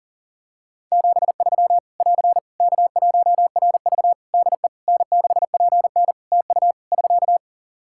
Morse code, heard as a keyed tone, about 43 wpm, 700 Hz. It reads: 73 <AR> K1RV DE N6PN TU <SK>
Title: Practice QSO - Exchange 4 - Ending - 40wpm